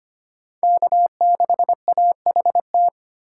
Morse code, heard as a keyed tone, about 25 wpm, 700 Hz. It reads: K6AHT
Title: USA Call Signs - 5 Characters - 3x Character Spacing - 25wpm